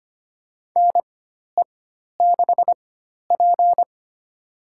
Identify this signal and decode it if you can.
Morse code, heard as a keyed tone, about 25 wpm, spaced as for 12 wpm, 700 Hz. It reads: NE6P
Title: USA Call Signs - 4 Characters - 4x Character Spacing - 25wpm